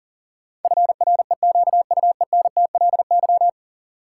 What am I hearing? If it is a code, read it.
Morse code, heard as a keyed tone, about 40 wpm, 700 Hz. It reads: FREQUENTLY
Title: Sentences from Top 1300 Words - New Word Drill Alt - 40wpm